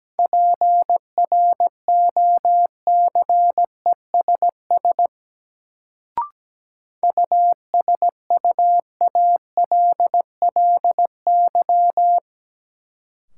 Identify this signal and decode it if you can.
Morse code, heard as a keyed tone, about 17 wpm, 700 Hz. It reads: PROCESS USUALLY
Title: Sentences from Top 1000 Words - Review Part 3 - New Word Drill - 17wpm